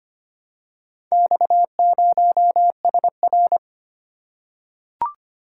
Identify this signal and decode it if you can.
Morse code, heard as a keyed tone, about 25 wpm, 700 Hz. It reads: X0SR
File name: practice / 25wpm / Four Letter-Number Combinations - 25wpm